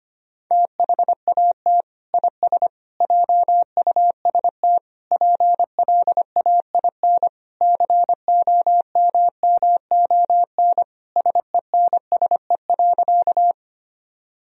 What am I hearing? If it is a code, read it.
Morse code, heard as a keyed tone, about 25 wpm, 700 Hz. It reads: THAT IS JUST PLAIN COMMON SENSE.
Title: Sentences from Top 1000 Words - Review Part 4 - 25wpm